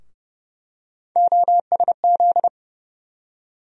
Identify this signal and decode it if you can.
Morse code, heard as a keyed tone, about 30 wpm, 700 Hz. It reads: OSZ
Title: USA Call Sign Postfixes - 3 Letters 30wpm